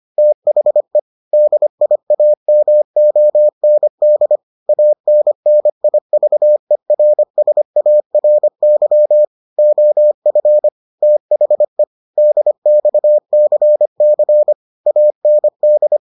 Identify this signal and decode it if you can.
Morse code, heard as a keyed tone, about 25 wpm, 600 Hz. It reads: THE DIAMOND ANNIVERSARY OF THE DXCC AND